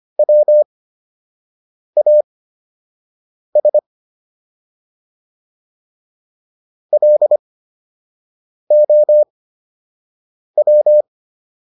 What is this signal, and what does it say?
Morse code, heard as a keyed tone, about 25 wpm, spaced as for 6 wpm, 600 Hz. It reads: WAS LOW